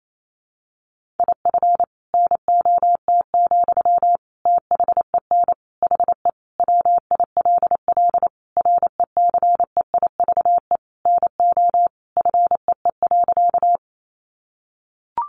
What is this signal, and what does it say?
Morse code, heard as a keyed tone, about 28 wpm, 700 Hz. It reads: IF NOT, THEN HE WILL RECEIVE NO FEE.
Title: Sentences from Top 1500 Words 28wpm